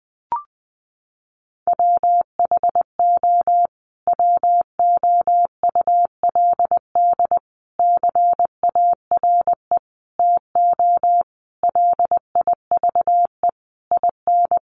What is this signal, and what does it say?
Morse code, heard as a keyed tone, about 20 wpm, 700 Hz. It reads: WHO WOULD CARE TO LIVE IN
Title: Sentences from Top 1500 Words 20wpm